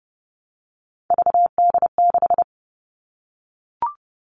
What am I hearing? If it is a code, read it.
Morse code, heard as a keyed tone, about 30 wpm, 700 Hz. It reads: VD6
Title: Three Challenging Letter-Number Combinations 30wpm